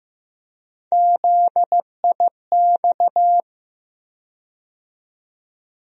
Morse code, heard as a keyed tone, about 15 wpm, 700 Hz. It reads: ZIX
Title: USA Call Sign Postfixes - 3 Letters 15wpm